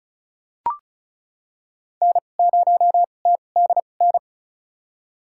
Morse code, heard as a keyed tone, about 36 wpm, spaced as for 25 wpm, 700 Hz. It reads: N0TDN
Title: USA Call Signs - 2x Character Spacing - 35wpm